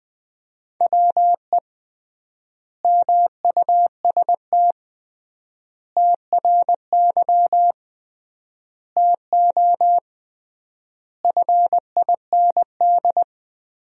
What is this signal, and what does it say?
Morse code, heard as a keyed tone, about 20 wpm, 700 Hz. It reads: WE MUST TRY TO FIND
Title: 3x Word-Spacing - Sentences from Top 400 Words - 20wpm